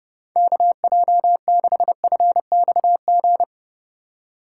Morse code, heard as a keyed tone, about 38 wpm, 700 Hz. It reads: KJ6FXG